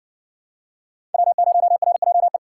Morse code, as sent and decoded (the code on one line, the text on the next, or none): .-- ----- .-. .--- .
W0RJE